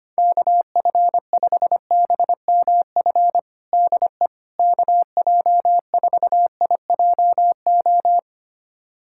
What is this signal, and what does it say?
Morse code, heard as a keyed tone, about 25 wpm, 700 Hz. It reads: KF5BMF DE KJ4IJO
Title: USA Callsign DE Callsign - 6 Characters - 25wpm